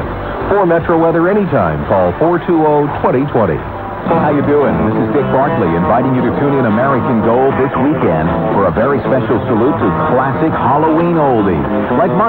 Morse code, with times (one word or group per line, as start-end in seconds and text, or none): none